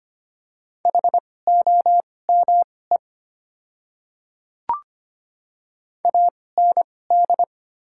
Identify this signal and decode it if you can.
Morse code, heard as a keyed tone, about 25 wpm, spaced as for 18 wpm, 690 Hz. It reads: HOME AND
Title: 2x Character Spacing - Top 200 Words - 25wpm